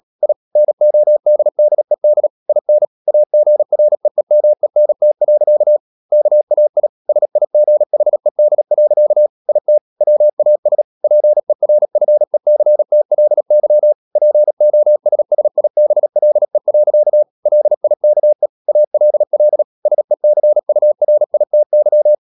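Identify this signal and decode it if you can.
Morse code, heard as a keyed tone, about 37 wpm, 600 Hz. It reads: I NODDED IN AGREEMENT. KAI SIGHED. IT WAS PERFECTLY POSSIBLE. LIKE ALL SECURITY
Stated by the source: Morse code practice at 35–39 wpm sending English text of Ken Follett's novel Never, from CW med Gnister.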